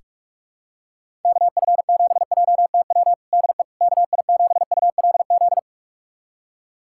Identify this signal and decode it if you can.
Morse code, heard as a keyed tone, about 46 wpm, 700 Hz. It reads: KF7JTW DE KI7ULZ
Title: USA Callsign DE Callsign 45wpm